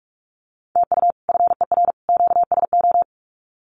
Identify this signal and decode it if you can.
Morse code, heard as a keyed tone, about 44 wpm, 700 Hz. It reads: TU FER QSO
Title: Common QSO Phrases 45wpm